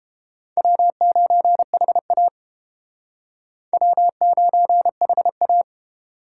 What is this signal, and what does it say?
Morse code, heard as a keyed tone, about 31 wpm, 700 Hz. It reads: W9HA W9HA